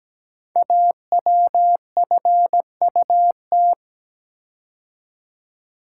Morse code, heard as a keyed tone, about 17 wpm, 700 Hz. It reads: AWFUT
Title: Code Groups - 5 Letters - 17wpm